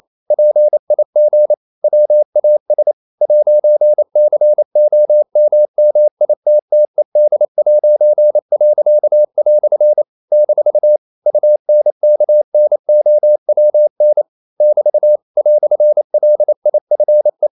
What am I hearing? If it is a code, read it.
Morse code, heard as a keyed tone, about 28 wpm, 600 Hz. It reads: PIG WAS 'COMMITTED'." - UNKNOWN <BT> "LIFE
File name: CW / Practice / EnglishQuotations0128